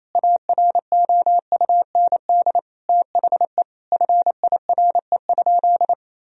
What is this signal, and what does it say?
Morse code, heard as a keyed tone, about 27 wpm, 700 Hz. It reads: AROUND THE FIRE?
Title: Sentences from Top 1000 Words - Review Part 3 - 28wpm